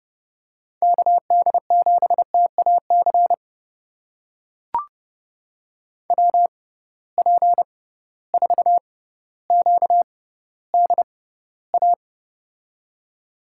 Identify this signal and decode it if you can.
Morse code, heard as a keyed tone, about 32 wpm, spaced as for 16 wpm, 700 Hz. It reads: KD7TAC WP4QDA
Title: USA Call Signs - 6 Characters - 6x Character Spacing - 30wpm